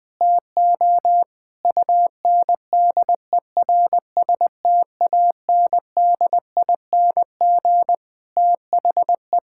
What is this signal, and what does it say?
Morse code, heard as a keyed tone, about 20 wpm, 700 Hz. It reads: TO UNDERSTANDING THE
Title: Sentences with 9-Word Phrases - Spoken Beginning - 20wpm